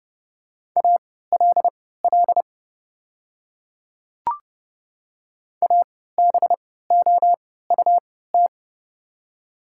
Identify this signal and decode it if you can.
Morse code, heard as a keyed tone, about 30 wpm, spaced as for 17 wpm, 700 Hz. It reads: ALL ABOUT
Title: 3x Character Spacing - Top 100 Words - 30wpm